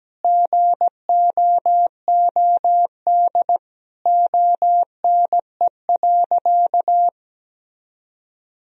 Morse code, heard as a keyed tone, about 17 wpm, 700 Hz. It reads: GOOD ONE.
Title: Sentences from Top 1000 Words - Review Part 3 - 17wpm